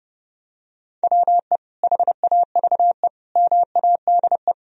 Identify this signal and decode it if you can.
Morse code, heard as a keyed tone, about 30 wpm, 700 Hz. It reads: WE HAVE MADE
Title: Sentences from Top 1000 Words - Review Part 3 - 30wpm